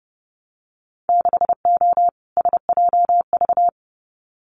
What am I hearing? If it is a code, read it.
Morse code, heard as a keyed tone, about 30 wpm, 700 Hz. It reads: 6O SJV
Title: Winter Field Day Exchanges Encore 30wpm